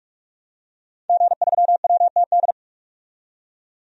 Morse code, heard as a keyed tone, about 46 wpm, 700 Hz. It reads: G3WTD
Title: DXCC-100 Call Signs - 5 Characters - 45wpm